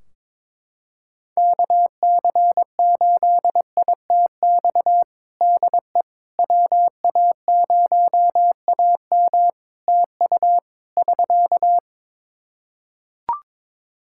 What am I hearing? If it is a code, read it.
Morse code, heard as a keyed tone, about 22 wpm, 700 Hz. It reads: KC8ITX DE WA0AM TU <SK>
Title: Practice QSO - Exchange 4 - Ending - 22wpm